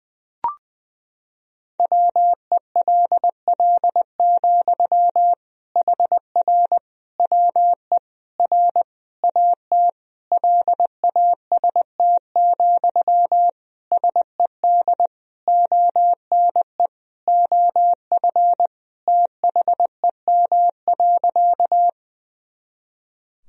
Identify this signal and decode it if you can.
Morse code, heard as a keyed tone, about 20 wpm, 700 Hz. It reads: WELL, HR WE R AT LAST, SED ONE OF THEM.
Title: Sentences from Top 300 Words and Abbreviations 20wpm